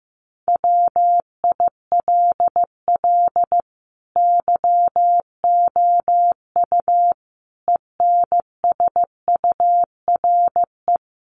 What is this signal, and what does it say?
Morse code, heard as a keyed tone, about 15 wpm, 700 Hz. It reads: WILL YOU ENSURE
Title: Sentences from Top 1100 Words 15wpm